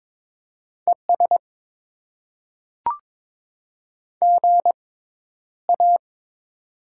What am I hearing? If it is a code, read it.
Morse code, heard as a keyed tone, about 21 wpm, spaced as for 11 wpm, 700 Hz. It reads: ES GA
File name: Top 20 QSO Elements - 6x Character Spacing - 22wpm